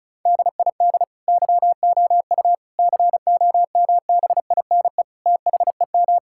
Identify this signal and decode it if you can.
Morse code, heard as a keyed tone, about 36 wpm, 700 Hz. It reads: DID YOU COMBINE THEM
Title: Sentences from Top 1600 Words 35wpm